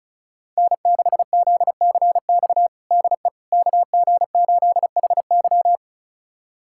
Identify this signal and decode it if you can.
Morse code, heard as a keyed tone, about 34 wpm, 700 Hz. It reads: N6ZCX DE KG8HY
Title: USA Callsign DE Callsign 35wpm